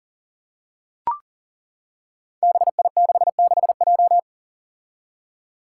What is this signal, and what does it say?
Morse code, heard as a keyed tone, about 39 wpm, 700 Hz. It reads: BI66J
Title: Five Challenging Letter-Number Combinations 40wpm